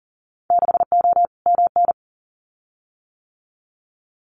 Morse code, heard as a keyed tone, about 40 wpm, 700 Hz. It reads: 6O MN
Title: Winter Field Day Exchanges Encore 40wpm